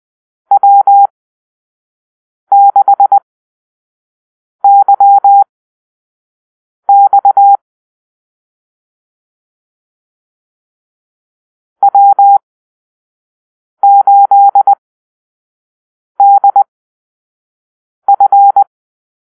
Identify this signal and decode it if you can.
Morse code, heard as a keyed tone, about 20 wpm, spaced as for 5 wpm, 800 Hz. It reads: W6YX W8DF
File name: Call-Sign Drill